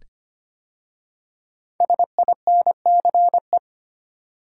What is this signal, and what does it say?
Morse code, heard as a keyed tone, about 25 wpm, 700 Hz. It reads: SINCE
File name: 5 Letter Words - First 3 Letters Given - 25wpm